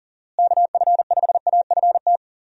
Morse code, heard as a keyed tone, about 40 wpm, 700 Hz. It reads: KF5AFT
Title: USA Call Signs - 6 Characters - Repeated 3x - 40wpm